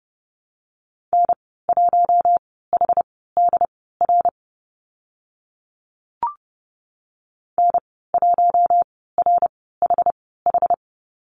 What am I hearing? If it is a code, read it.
Morse code, heard as a keyed tone, about 30 wpm, spaced as for 17 wpm, 700 Hz. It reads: N1HDR N1RHH
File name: Lesson 14g - Number 2 - Call Signs 30wpm